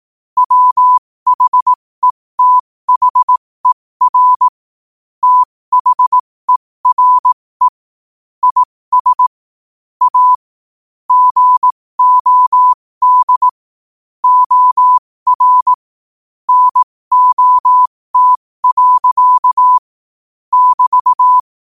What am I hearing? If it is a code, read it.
Morse code, heard as a keyed tone, about 18 wpm, spaced as for 15 wpm, 1000 Hz. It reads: WHETHER THERE IS A GOD OR NOT. <BT>
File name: qotdfiles / 15wpm